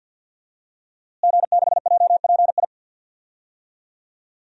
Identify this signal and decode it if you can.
Morse code, heard as a keyed tone, about 51 wpm, 700 Hz. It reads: G6JPI
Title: DXCC-100 Call Signs - 5 Characters - Repeated 3x - 50wpm